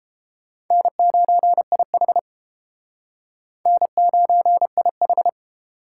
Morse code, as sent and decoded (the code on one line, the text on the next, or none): -. ----. .. .... / -. ----. .. ....
N9IH N9IH